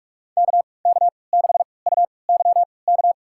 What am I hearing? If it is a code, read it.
Morse code, heard as a keyed tone, about 43 wpm, spaced as for 25 wpm, 700 Hz. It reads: KK6UYK
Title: USA Call Signs - 3x Character Spacing - 45wpm